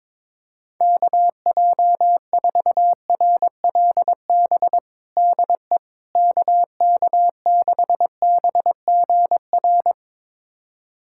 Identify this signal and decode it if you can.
Morse code, heard as a keyed tone, about 22 wpm, 700 Hz. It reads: KJ4RLB DE KK6BGR
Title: USA Callsign DE Callsign - 6 Characters - 22wpm